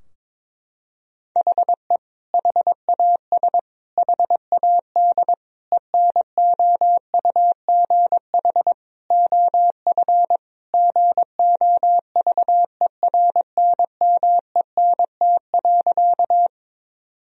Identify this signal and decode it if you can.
Morse code, heard as a keyed tone, about 22 wpm, 700 Hz. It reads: HE HAS HAD ENOUGH OF GOVERNMENT.
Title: Sentences from Top 1000 Words - Review Part 3 - 22wpm